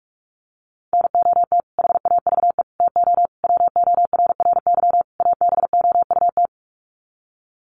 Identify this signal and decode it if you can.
Morse code, heard as a keyed tone, about 44 wpm, 700 Hz. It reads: NOT HAVE TO WORRY ABOUT
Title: Sets of 5 Words 45wpm